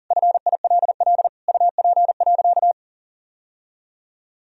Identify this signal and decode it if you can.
Morse code, heard as a keyed tone, about 40 wpm, 700 Hz. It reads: FILL UP.
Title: Sentences from Top 1000 Words - Review Part 3 - 40wpm